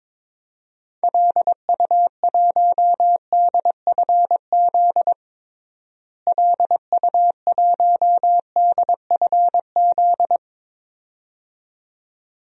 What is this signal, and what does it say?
Morse code, heard as a keyed tone, about 22 wpm, 700 Hz. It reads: LU1DFZ LU1DFZ
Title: DXCC-100 Call Signs - 6 Characters - Repeated 2x - 22wpm